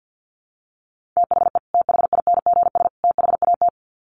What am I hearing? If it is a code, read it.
Morse code, heard as a keyed tone, about 50 wpm, 700 Hz. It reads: THE THINGS THAT
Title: Sets of 3 Words 50wpm